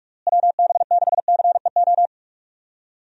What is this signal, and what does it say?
Morse code, heard as a keyed tone, about 43 wpm, 700 Hz. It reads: WB6CEO
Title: USA Call Signs - 6 Characters - Repeated 3x - 45wpm